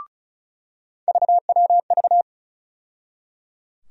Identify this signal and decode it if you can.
Morse code, heard as a keyed tone, about 36 wpm, 700 Hz. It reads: VWV
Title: USA Call Sign Postfixes - 3 Letters 35wpm